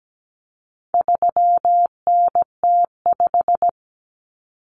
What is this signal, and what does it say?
Morse code, heard as a keyed tone, about 17 wpm, 700 Hz. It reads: 3NT5
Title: Common Cut Numbers - 4-Digit 17wpm